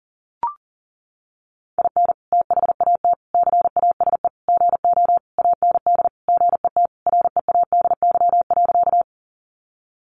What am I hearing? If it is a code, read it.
Morse code, heard as a keyed tone, about 40 wpm, 700 Hz. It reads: IN THAT CASE GO AND GET READY.
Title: Sentences from Top 700 Words 40wpm